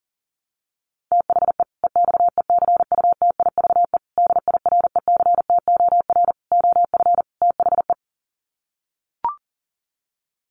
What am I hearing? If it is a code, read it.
Morse code, heard as a keyed tone, about 40 wpm, 700 Hz. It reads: THE EXECUTIVE DIRECTOR OF THE
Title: Sets of 5 Words Encore 40wpm